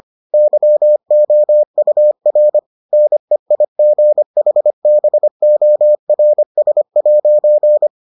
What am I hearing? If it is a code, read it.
Morse code, heard as a keyed tone, about 25 wpm, 600 Hz. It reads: YOUR NEIGHBORS'